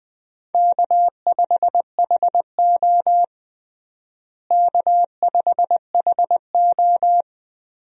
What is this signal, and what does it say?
Morse code, heard as a keyed tone, about 20 wpm, 700 Hz. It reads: K5HO K5HO
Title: USA Call Signs - 4 Characters - Repeated 3x - 20wpm